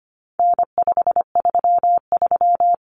Morse code, heard as a keyed tone, about 25 wpm, 700 Hz. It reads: N533
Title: Common Cut Numbers - 4-Digit 25wpm